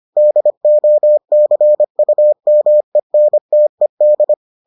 Morse code, heard as a keyed tone, about 25 wpm, 600 Hz. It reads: DOCUMENTED